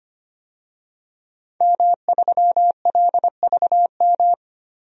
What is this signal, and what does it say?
Morse code, heard as a keyed tone, about 25 wpm, 700 Hz. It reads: M3LVM
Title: DXCC-100 Call Signs - 5 Characters - Repeated 2x - 25wpm